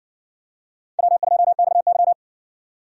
Morse code, heard as a keyed tone, about 60 wpm, 700 Hz. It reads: W2XY